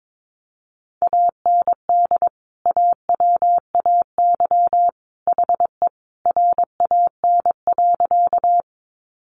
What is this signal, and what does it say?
Morse code, heard as a keyed tone, about 22 wpm, 700 Hz. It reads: AND AWAY HE RAN.
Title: Sentences from Top 600 Words 22wpm